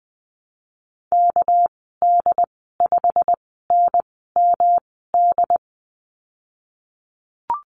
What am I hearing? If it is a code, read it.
Morse code, heard as a keyed tone, about 20 wpm, spaced as for 15 wpm, 700 Hz. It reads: KD5NMD
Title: 231 Lesson 37g - Letter x - Call Signs 20wpm